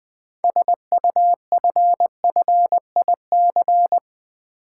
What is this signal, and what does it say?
Morse code, heard as a keyed tone, about 20 wpm, 700 Hz. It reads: SUFFIC
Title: Build A Word - 6 Letters At A Time - 12-Letter Words - 20wpm